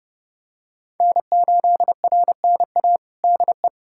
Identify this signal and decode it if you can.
Morse code, heard as a keyed tone, about 30 wpm, 700 Hz. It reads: N8RNA DE
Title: USA Callsign DE Callsign 30wpm